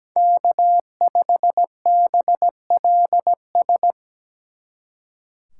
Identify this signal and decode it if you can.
Morse code, heard as a keyed tone, about 20 wpm, 700 Hz. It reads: K5BLS
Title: Speed Racing - USA 1x3 Call Signs 17wpm